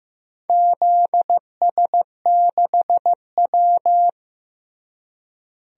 Morse code, heard as a keyed tone, about 15 wpm, 700 Hz. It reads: ZS6W
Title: DXCC-100 Call Signs - 4 Characters - Repeated 3x - 15wpm